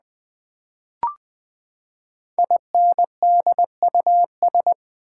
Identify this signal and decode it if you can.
Morse code, heard as a keyed tone, about 20 wpm, 700 Hz. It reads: INDUS
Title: Build A Word - 5 Letters At A Time - 10-Letter Words - 20wpm